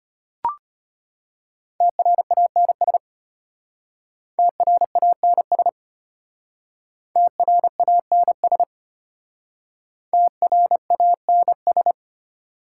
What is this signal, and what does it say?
Morse code, heard as a keyed tone, about 32 wpm, 700 Hz. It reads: TRANS TRANS TRANS TRANS